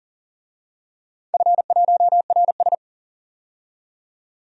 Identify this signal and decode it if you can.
Morse code, heard as a keyed tone, about 38 wpm, 700 Hz. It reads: F1RS
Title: DXCC-100 Call Signs - 4 Characters - 40wpm